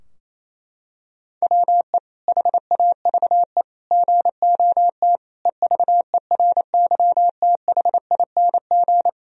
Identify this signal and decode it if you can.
Morse code, heard as a keyed tone, about 28 wpm, 700 Hz. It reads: WE HAVE GOT EVERYTHING
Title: Sentences from Top 1000 Words - Review Part 3 - 28wpm